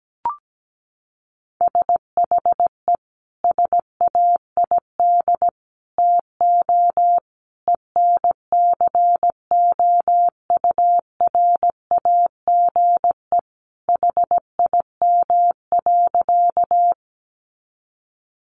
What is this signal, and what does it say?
Morse code, heard as a keyed tone, about 17 wpm, 700 Hz. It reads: SHE SAID TO ENCOURAGE HIM.
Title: Sentences from Top 1400 Words 17wpm